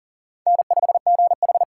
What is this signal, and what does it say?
Morse code, heard as a keyed tone, about 40 wpm, 700 Hz. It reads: N5GH
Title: USA Call Signs - 4 Characters - 4x Character Spacing - 40wpm